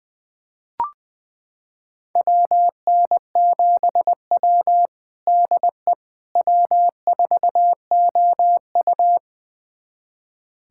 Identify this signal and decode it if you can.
Morse code, heard as a keyed tone, about 20 wpm, 700 Hz. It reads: WN7W DE W4OU
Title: USA Callsign DE Callsign - 4 Characters - 20wpm